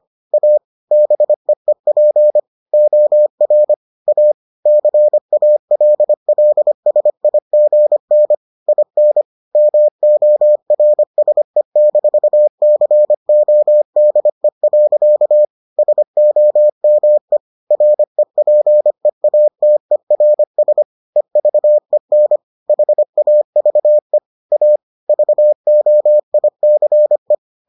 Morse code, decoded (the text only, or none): A BEEP OR A CALLSIGN IN MORSE-CODE. SOME REPEATERS EVEN HAVE A VOICE